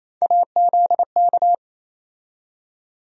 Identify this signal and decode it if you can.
Morse code, heard as a keyed tone, about 28 wpm, 700 Hz. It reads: AZK
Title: USA Call Sign Postfixes - 3 Letters 28wpm